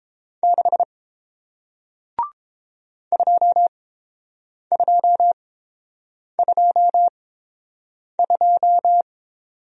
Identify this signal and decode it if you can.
Morse code, heard as a keyed tone, about 30 wpm, spaced as for 7 wpm, 700 Hz. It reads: 6 2222